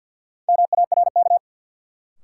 Morse code, heard as a keyed tone, about 53 wpm, 700 Hz. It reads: MARK
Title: Sentences from Top 300 Words - New Word Drill - 50wpm